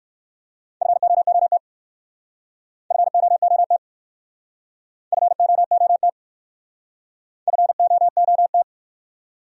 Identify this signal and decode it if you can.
Morse code, heard as a keyed tone, about 60 wpm, 700 Hz. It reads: FOOT FOOT FOOT FOOT